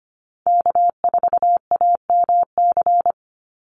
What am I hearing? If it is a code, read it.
Morse code, heard as a keyed tone, about 25 wpm, 700 Hz. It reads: K4AMC
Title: Call Signs Repeated 2x - 25wpm